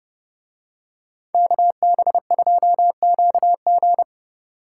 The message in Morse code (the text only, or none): KB2QG